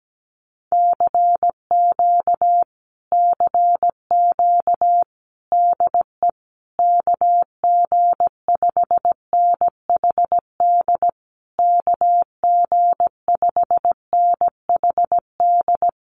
Morse code, heard as a keyed tone, about 17 wpm, 700 Hz. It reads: CQ CQ DE KG5NHD KG5NHD
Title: Common QSO Phrases Encore 17wpm